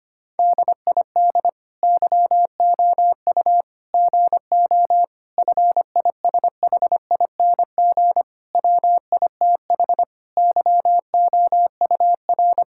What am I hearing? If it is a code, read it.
Morse code, heard as a keyed tone, about 25 wpm, 700 Hz. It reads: DID YOU GO FISHING WITH YOUR
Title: Sentences from Top 1000 Words - Review Part 5 - 25wpm